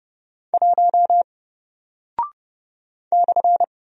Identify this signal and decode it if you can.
Morse code, heard as a keyed tone, about 30 wpm, 700 Hz. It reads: 1 /
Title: Single Character (Letter:Number:Punctuation:Prosign) 30wpm